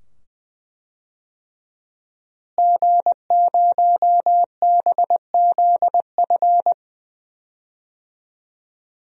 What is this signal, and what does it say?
Morse code, heard as a keyed tone, about 20 wpm, 700 Hz. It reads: G0BZF
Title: DXCC-100 Call Signs - 5 Characters - Repeated 2x - 20wpm